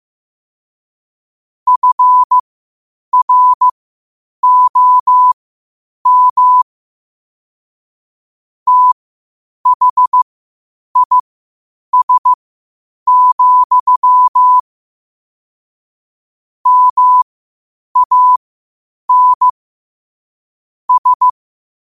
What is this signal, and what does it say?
Morse code, heard as a keyed tone, about 15 wpm, spaced as for 8 wpm, 1000 Hz. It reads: FROM THIS, MAN S
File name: qotdfiles / 8wpm